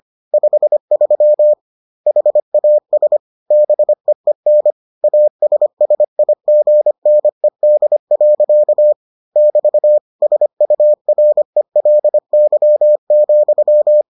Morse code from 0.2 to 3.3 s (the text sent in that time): 53 HAS